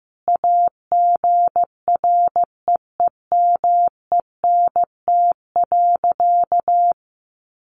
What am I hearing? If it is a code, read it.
Morse code, heard as a keyed tone, about 14 wpm, 700 Hz. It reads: AGREEMENT.
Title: Sentences from Top 1500 Words 15wpm